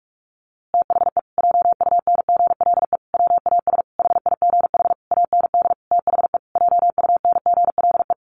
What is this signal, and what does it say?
Morse code, heard as a keyed tone, about 44 wpm, 700 Hz. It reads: THE JUNGLE WAS HIGH AND THE JUNGLE
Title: A Sound Of Thunder 45wpm